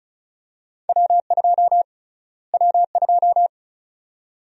Morse code, heard as a keyed tone, about 34 wpm, 700 Hz. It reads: W2 W2
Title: USA Call Sign Prefixes - Repeated 2x - 35wpm